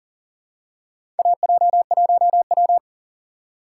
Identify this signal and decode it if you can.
Morse code, heard as a keyed tone, about 40 wpm, 700 Hz. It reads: AJ1W
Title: USA Call Signs - 4 Characters - 4x Character Spacing - 40wpm